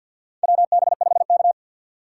Morse code, heard as a keyed tone, about 51 wpm, 700 Hz. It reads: WB5K
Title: K1USN SST QSO Practice - 50wpm